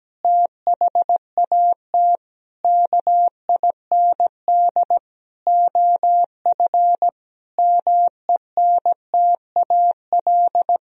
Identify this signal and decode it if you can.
Morse code, heard as a keyed tone, about 17 wpm, 700 Hz. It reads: THAT KIND OF MENTAL
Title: Sentences from Top 1000 Words - Review Part 4 - 17wpm